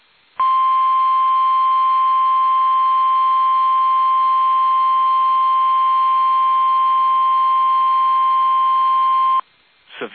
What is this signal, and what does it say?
Emergency Alert System (SAME) tones (the 1050 Hz attention tone). It is heard from 1.1 to 9.4 s.
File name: eas / KUSO / KUSO